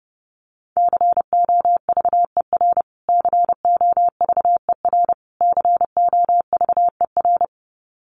Morse code, heard as a keyed tone, about 30 wpm, 700 Hz. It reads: COVER COVER COVER
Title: Top 500 Words Repeated 30wpm